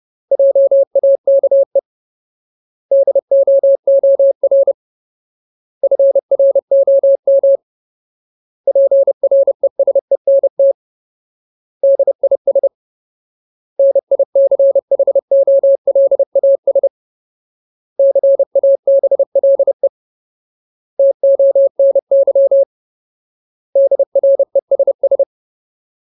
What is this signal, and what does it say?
Morse code, heard as a keyed tone, about 30 wpm, 550 Hz. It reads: JAKE DOOR FROM PRESENT DIS NICHOLAS CABLE TONY DRESS
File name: Ultimate Word List / 30WPM Ultimate